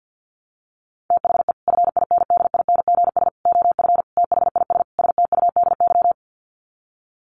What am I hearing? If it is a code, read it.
Morse code, heard as a keyed tone, about 48 wpm, 700 Hz. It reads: THE FINDINGS OF THIS STUDY
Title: Sets of 5 Words Encore2 50wpm